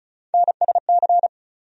Morse code, heard as a keyed tone, about 36 wpm, 700 Hz. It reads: NSC
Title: USA Call Sign Postfixes - 3 Letters 35wpm